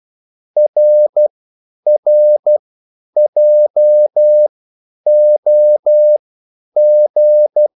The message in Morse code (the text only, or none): RRJOG